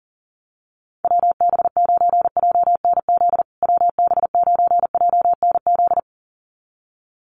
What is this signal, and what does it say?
Morse code, heard as a keyed tone, about 40 wpm, 700 Hz. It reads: WB9JNZ WB9JNZ
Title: USA Call Signs Repeated 2x 40wpm